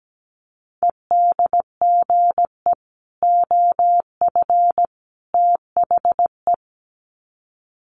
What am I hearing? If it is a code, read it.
Morse code, heard as a keyed tone, about 17 wpm, 700 Hz. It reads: EDGE OF THE
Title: Sets of 3 Words 17wpm